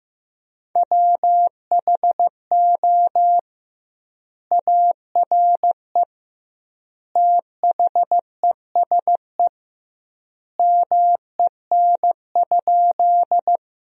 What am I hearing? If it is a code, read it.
Morse code, heard as a keyed tone, about 15 wpm, 700 Hz. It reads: WHO ARE THESE MEN?
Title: 2x Word-Spacing - Sentences from Top 200 Words - 15wpm